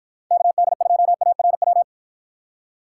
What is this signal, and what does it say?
Morse code, heard as a keyed tone, about 54 wpm, 700 Hz. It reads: KD2ARW